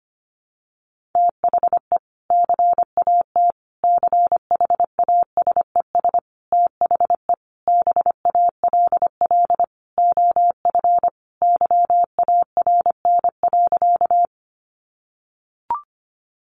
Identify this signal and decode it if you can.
Morse code, heard as a keyed tone, about 25 wpm, 700 Hz. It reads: THE CAT CHASES THE BALL OF YARN.
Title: Additional Easy Phrases 25wpm